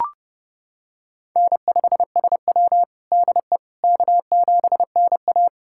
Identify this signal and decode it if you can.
Morse code, heard as a keyed tone, about 31 wpm, 700 Hz. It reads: N5SW DE K7NA
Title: USA Callsign DE Callsign - 4 Characters - 30wpm